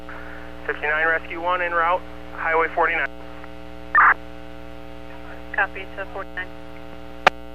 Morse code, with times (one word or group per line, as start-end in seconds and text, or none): none